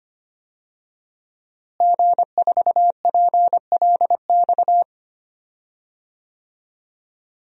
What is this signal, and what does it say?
Morse code, heard as a keyed tone, about 25 wpm, 700 Hz. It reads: G4PLX
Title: DXCC-100 Call Signs - 5 Characters - Repeated 3x - 25wpm